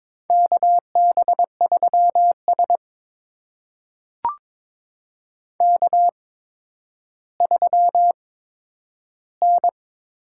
Morse code, heard as a keyed tone, about 22 wpm, spaced as for 11 wpm, 700 Hz. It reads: KB3S K3N
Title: USA Call Signs - 4 Characters - 8x Character Spacing - 22wpm